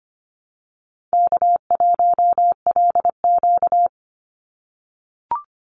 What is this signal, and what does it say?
Morse code, heard as a keyed tone, about 25 wpm, 700 Hz. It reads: K1LQ
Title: USA Call Signs 25wpm